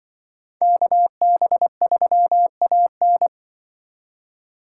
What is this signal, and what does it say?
Morse code, heard as a keyed tone, about 24 wpm, 700 Hz. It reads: KB3AN
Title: Speed Racing - USA 2x2 Call Signs 20wpm